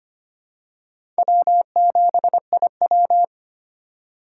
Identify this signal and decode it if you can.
Morse code, heard as a keyed tone, about 25 wpm, 700 Hz. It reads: W7IW
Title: USA Call Signs - 3x Character Spacing - 25wpm